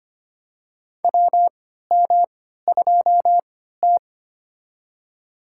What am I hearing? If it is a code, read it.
Morse code, heard as a keyed tone, about 25 wpm, spaced as for 14 wpm, 700 Hz. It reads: WM2T
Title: USA Call Signs - 4 Characters - 3x Character Spacing - 25wpm